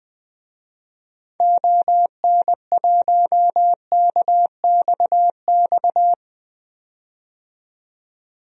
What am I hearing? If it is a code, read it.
Morse code, heard as a keyed tone, about 20 wpm, 700 Hz. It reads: ON1KXX
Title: DXCC-100 Call Signs - 6 Characters - 20wpm